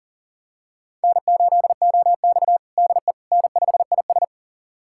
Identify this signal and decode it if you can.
Morse code, heard as a keyed tone, about 40 wpm, 700 Hz. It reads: N8OX DE N5IS
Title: USA Callsign DE Callsign - 4 Characters - 40wpm